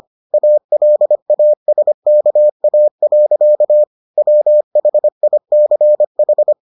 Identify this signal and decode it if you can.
Morse code, heard as a keyed tone, about 25 wpm, 600 Hz. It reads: ALASKA. WHICH